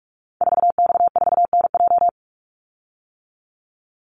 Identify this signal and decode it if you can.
Morse code, heard as a keyed tone, about 44 wpm, 700 Hz. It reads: 4X4NJ
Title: Call Signs - 45wpm